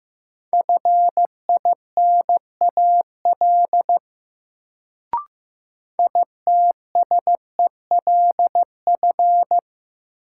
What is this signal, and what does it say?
Morse code, heard as a keyed tone, about 15 wpm, 700 Hz. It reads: FINAL ITSELF
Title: Sentences from Top 800 Words - New Word Drill - 15wpm